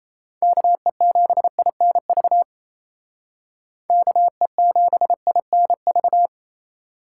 Morse code, heard as a keyed tone, about 34 wpm, 700 Hz. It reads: KE7INV KE7INV
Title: Speed Racing - USA 2x3 Call Signs 28wpm